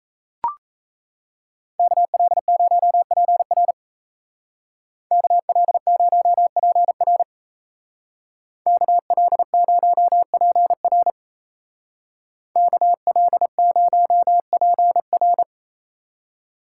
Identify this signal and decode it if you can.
Morse code, heard as a keyed tone, about 33 wpm, 700 Hz. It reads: KL0PR KL0PR KL0PR KL0PR
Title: Speed Racing - USA 2x2 Call Signs 28wpm